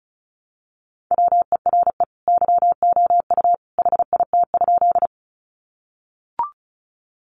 Morse code, heard as a keyed tone, about 34 wpm, 700 Hz. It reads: WERE YOU HIT?
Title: Sentences from Top 1200 Words 35wpm